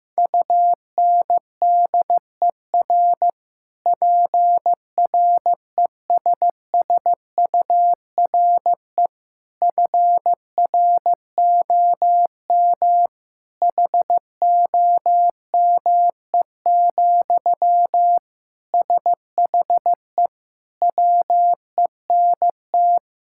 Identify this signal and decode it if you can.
Morse code, heard as a keyed tone, about 16 wpm, 700 Hz. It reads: UNDER PRESSURE FROM HOME, SHE WENT
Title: Sentences from Top 1000 Words - Review Part 6 - 15wpm